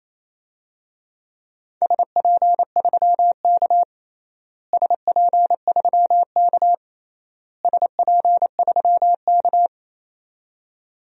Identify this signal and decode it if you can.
Morse code, heard as a keyed tone, about 28 wpm, 700 Hz. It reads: SP3K SP3K SP3K
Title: DXCC-100 Call Signs - 4 Characters - Repeated 3x - 28wpm